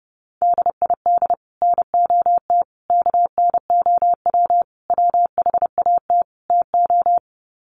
Morse code, heard as a keyed tone, about 30 wpm, 700 Hz. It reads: DID NOT KNOW WHAT TO
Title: Sets of 5 Words 30wpm